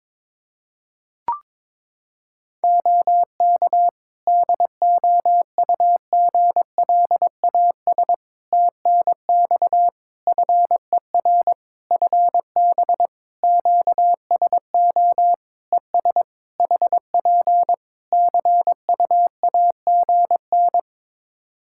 Morse code, heard as a keyed tone, about 22 wpm, 700 Hz. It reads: OK DOUGLAS TNX FER FB QSO ES HP CUAGN
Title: Practice QSO - Exchange 4 - Ending - 22wpm